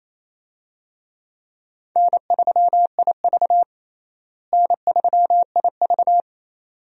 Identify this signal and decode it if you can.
Morse code, heard as a keyed tone, about 28 wpm, 700 Hz. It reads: N3IV N3IV
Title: USA Call Signs - 4 Characters - Repeated 2x - 28wpm